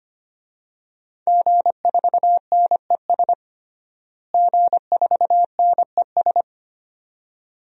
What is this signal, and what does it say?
Morse code, heard as a keyed tone, about 25 wpm, 700 Hz. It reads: G4NES G4NES
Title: DXCC-100 Call Signs - 5 Characters - Repeated 2x - 25wpm